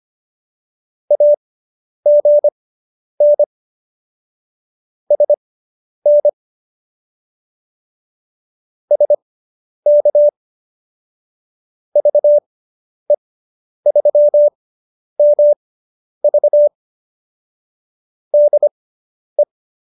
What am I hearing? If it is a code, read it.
Morse code, heard as a keyed tone, about 25 wpm, spaced as for 10 wpm, 590 Hz. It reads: AGN SN SK VE3MV DE